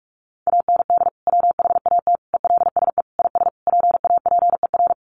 Morse code, heard as a keyed tone, about 44 wpm, 700 Hz. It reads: AND WHAT ELSE IS PAPER
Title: Sentences from Top 1500 Words 45wpm